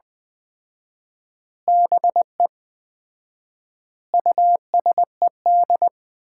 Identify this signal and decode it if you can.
Morse code, heard as a keyed tone, about 20 wpm, 700 Hz. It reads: BE USED
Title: 4x Word-Spacing - Sets of 4 Words - 20wpm